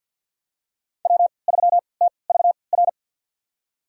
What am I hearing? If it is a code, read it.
Morse code, heard as a keyed tone, about 49 wpm, spaced as for 28 wpm, 700 Hz. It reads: W3TVR
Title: USA Call Signs - 5 Characters - 3x Character Spacing - 50wpm